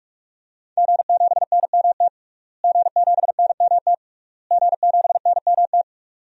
Morse code, heard as a keyed tone, about 42 wpm, 700 Hz. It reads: G7NMT G7NMT G7NMT
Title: DXCC-100 Call Signs - 5 Characters - Repeated 3x - 45wpm